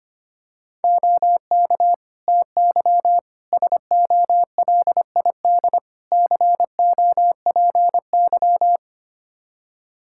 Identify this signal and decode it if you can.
Morse code, heard as a keyed tone, about 25 wpm, 700 Hz. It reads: OK TY SOLID COPY
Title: K1USN SST QSO Practice 25wpm